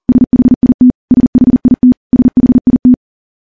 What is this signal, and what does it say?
Morse code, heard as a keyed tone, about 41 wpm, 250 Hz. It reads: SHIT SHIT SHIT
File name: morse / morse-training-40WPM